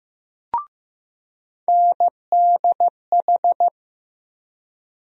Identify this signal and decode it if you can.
Morse code, heard as a keyed tone, about 15 wpm, 700 Hz. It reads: NDH
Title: USA Call Sign Postfixes - 3 Letters 15wpm